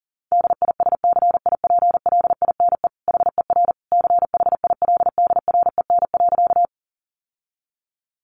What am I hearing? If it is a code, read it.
Morse code, heard as a keyed tone, about 40 wpm, 700 Hz. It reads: DISCIPLINE HER CHILDREN.
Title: Sentences from Top 900 Words 40wpm